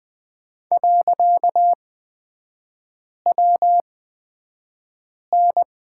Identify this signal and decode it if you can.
Morse code, heard as a keyed tone, about 20 wpm, spaced as for 5 wpm, 700 Hz. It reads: .WN
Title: 255 Lesson 41e - Period - Three Characters 20wpm